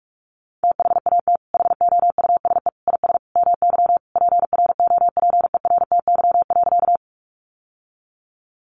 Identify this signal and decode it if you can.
Morse code, heard as a keyed tone, about 44 wpm, 700 Hz. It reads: THAT HOUSE IS MY PROPERTY.
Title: Sentences from Top 1000 Words 45wpm